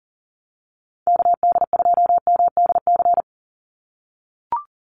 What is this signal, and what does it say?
Morse code, heard as a keyed tone, about 40 wpm, 700 Hz. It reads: KD2MDC
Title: USA Call Signs 40wpm